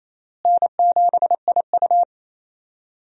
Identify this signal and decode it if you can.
Morse code, heard as a keyed tone, about 28 wpm, 700 Hz. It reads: N7IU
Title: USA Call Signs - 4 Characters - Repeated 3x - 28wpm